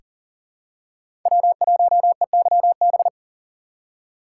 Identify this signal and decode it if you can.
Morse code, heard as a keyed tone, about 40 wpm, 700 Hz. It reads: W1EYB
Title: USA Call Signs - 5 Characters - 3x Character Spacing - 40wpm